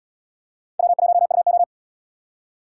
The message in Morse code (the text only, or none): W1UO